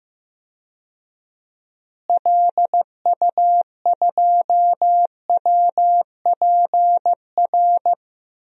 Morse code, heard as a keyed tone, about 15 wpm, 700 Hz. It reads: LU2WPR
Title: DXCC-100 Call Signs - 6 Characters - Repeated 2x - 15wpm